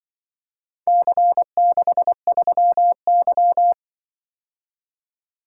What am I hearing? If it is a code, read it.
Morse code, heard as a keyed tone, about 25 wpm, 700 Hz. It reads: C63Y
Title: Speed-Racing - Four Letter-Number Combinations - 20wpm